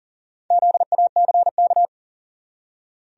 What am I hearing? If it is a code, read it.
Morse code, heard as a keyed tone, about 40 wpm, 700 Hz. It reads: ZACK
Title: USA Common Names - Max 4 Letters - 40wpm